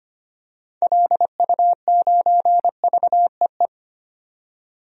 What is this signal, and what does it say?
Morse code, heard as a keyed tone, about 25 wpm, 700 Hz. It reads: LU9VEE
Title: DXCC-100 Call Signs - 6 Characters - 25wpm